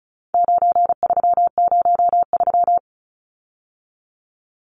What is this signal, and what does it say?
Morse code, heard as a keyed tone, about 34 wpm, 700 Hz. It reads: 9303
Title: Numbers - 4-Digit 35wpm